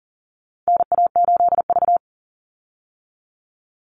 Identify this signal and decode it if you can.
Morse code, heard as a keyed tone, about 40 wpm, 700 Hz. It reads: NA8V
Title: Call Signs - 40wpm